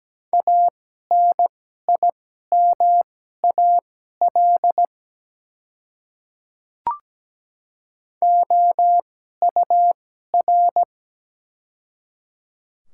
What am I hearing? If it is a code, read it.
Morse code, heard as a keyed tone, about 17 wpm, spaced as for 12 wpm, 700 Hz. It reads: ANIMAL OUR
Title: 2x Character Spacing - Top 200 Words - 17wpm